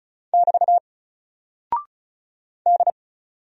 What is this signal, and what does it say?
Morse code, heard as a keyed tone, about 36 wpm, 700 Hz. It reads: <BT> D
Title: Single Character (Letter:Number:Punctuation:Prosign) 35wpm